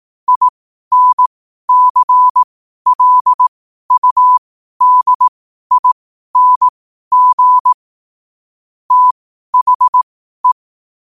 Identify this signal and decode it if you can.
Morse code, heard as a keyed tone, about 18 wpm, spaced as for 12 wpm, 1000 Hz. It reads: INCLUDING THE
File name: qotdfiles / 12wpm